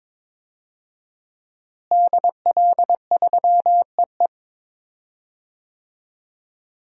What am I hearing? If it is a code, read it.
Morse code, heard as a keyed tone, about 22 wpm, 700 Hz. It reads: DL3EE
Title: DXCC-100 Call Signs - 5 Characters - Repeated 3x - 22wpm